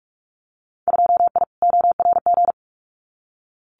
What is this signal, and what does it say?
Morse code, heard as a keyed tone, about 44 wpm, 690 Hz. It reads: 2I ORG
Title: Winter Field Day Exchanges Encore 45wpm